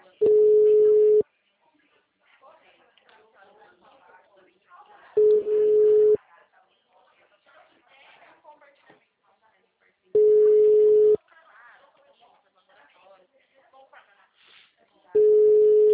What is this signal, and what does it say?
Ringback tone (Europe, 425 Hz), with speech.
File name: out- -2027- - - .17127